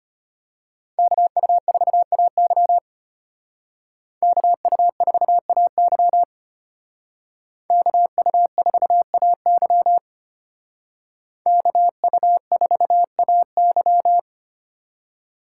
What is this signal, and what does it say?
Morse code, heard as a keyed tone, about 31 wpm, 700 Hz. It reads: KU4AY KU4AY KU4AY KU4AY
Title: Speed Racing - USA 2x2 Call Signs 25wpm